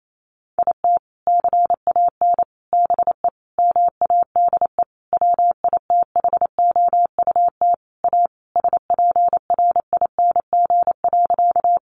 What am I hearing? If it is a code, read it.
Morse code, heard as a keyed tone, about 28 wpm, 700 Hz. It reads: IT CAN BE MADE WITHOUT A SPRING.
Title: Sentences from Top 1200 Words 28wpm